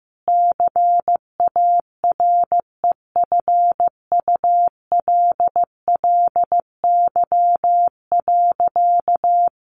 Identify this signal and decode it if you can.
Morse code, heard as a keyed tone, about 15 wpm, 700 Hz. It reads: CAREFULLY.
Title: Sentences from Top 1500 Words 15wpm